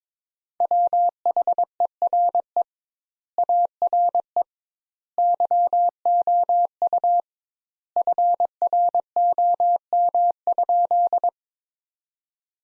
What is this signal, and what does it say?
Morse code, heard as a keyed tone, about 22 wpm, 700 Hz. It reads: WHERE ARE YOU FROM?
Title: Sentences Top 200 Words - 22wpm